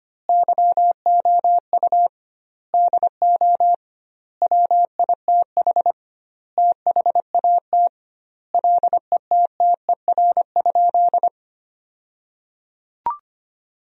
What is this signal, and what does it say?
Morse code, heard as a keyed tone, about 25 wpm, 700 Hz. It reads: YOU DO WITH THAT LETTER?
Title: 2x Word-Spacing - Sentences from Top 300 Words - 25wpm